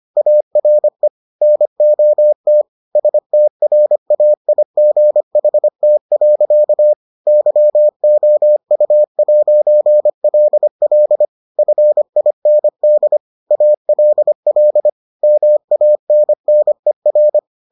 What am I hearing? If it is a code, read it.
Morse code, heard as a keyed tone, about 25 wpm, 600 Hz. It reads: ARE NOT STRAIGHT. YOU'LL FIND ALL MANNER